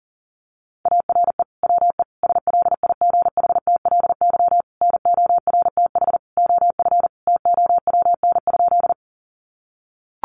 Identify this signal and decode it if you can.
Morse code, heard as a keyed tone, about 40 wpm, 700 Hz. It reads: ARE WE SLIGHTLY NORTH OF TOWN?
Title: Sentences from Top 1300 Words 40wpm